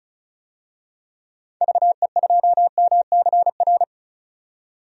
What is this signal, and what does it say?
Morse code, heard as a keyed tone, about 36 wpm, 700 Hz. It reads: VE2MCR
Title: DXCC-100 Call Signs - 6 Characters - 35wpm